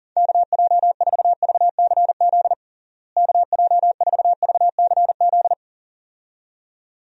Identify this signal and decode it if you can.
Morse code, heard as a keyed tone, about 37 wpm, 700 Hz. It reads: KJ4VCZ KJ4VCZ
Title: USA Call Signs - 6 Characters - Repeated 2x - 40wpm